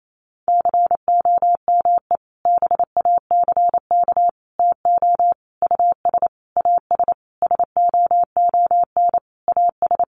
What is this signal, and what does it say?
Morse code, heard as a keyed tone, about 28 wpm, 700 Hz. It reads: COME BACK TO US AS SOON AS
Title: Sentences from Top 500 Words 28wpm